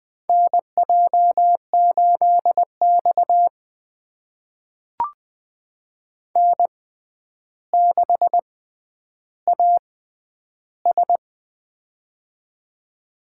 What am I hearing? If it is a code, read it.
Morse code, heard as a keyed tone, about 20 wpm, spaced as for 10 wpm, 700 Hz. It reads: NJ8X N6AS
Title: USA Call Signs - 4 Characters - 6x Character Spacing - 20wpm